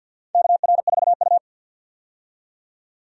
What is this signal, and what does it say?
Morse code, heard as a keyed tone, about 52 wpm, 700 Hz. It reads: KR4U
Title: USA Call Signs - 4 Characters - 50wpm